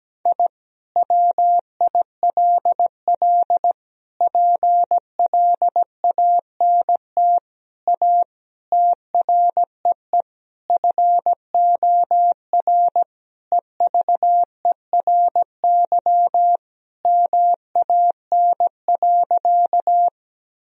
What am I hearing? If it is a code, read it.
Morse code, heard as a keyed tone, about 17 wpm, 700 Hz. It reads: I WILL PLANT A TREE FOR EVERY MAN.
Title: Sentences from Top 1000 Words - Review Part 1 - 17wpm